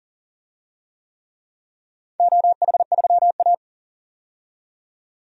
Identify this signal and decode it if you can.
Morse code, heard as a keyed tone, about 40 wpm, 700 Hz. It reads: OH3A
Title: DXCC-100 Call Signs - 4 Characters - 40wpm